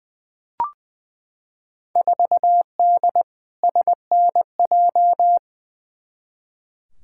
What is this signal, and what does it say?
Morse code, heard as a keyed tone, about 20 wpm, 700 Hz. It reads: 4D SNJ
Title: Field Day Exchanges Encore - 20wpm